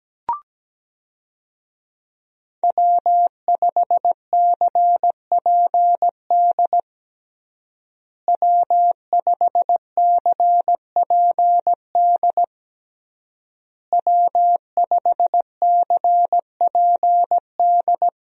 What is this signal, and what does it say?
Morse code, heard as a keyed tone, about 17 wpm, 700 Hz. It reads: W5CPD W5CPD W5CPD
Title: USA Call Signs - 5 Characters - Repeated 3x - 17wpm